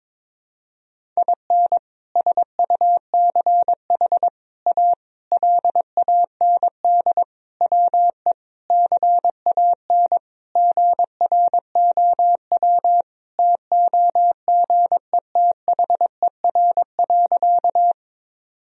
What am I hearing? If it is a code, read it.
Morse code, heard as a keyed tone, about 22 wpm, 700 Hz. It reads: IN SUCH A LAND WE CAN GROW TOGETHER.
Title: Sentences from Top 1000 Words - Review Part 1 - 22wpm